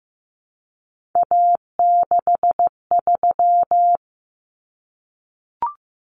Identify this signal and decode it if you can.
Morse code, heard as a keyed tone, about 15 wpm, 700 Hz. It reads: A63
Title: Common Cut Numbers - 3-Digit 15wpm